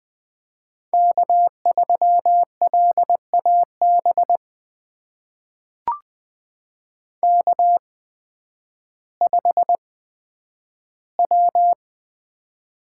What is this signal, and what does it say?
Morse code, heard as a keyed tone, about 20 wpm, spaced as for 11 wpm, 700 Hz. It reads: K3LAB K5W
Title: USA Call Signs - 5 Characters - 8x Character Spacing - 20wpm